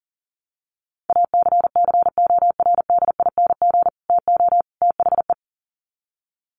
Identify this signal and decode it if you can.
Morse code, heard as a keyed tone, about 40 wpm, 700 Hz. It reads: ACCORDING TO THE
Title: Sets of 3 Words 40wpm